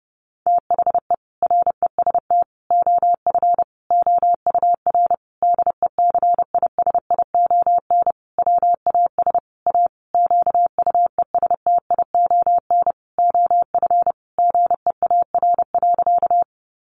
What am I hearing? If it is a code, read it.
Morse code, heard as a keyed tone, about 30 wpm, 700 Hz. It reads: THE REST OF OUR DECISION WAS A QUESTION OF GEAR.
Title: Sentences from Top 1400 Words 30wpm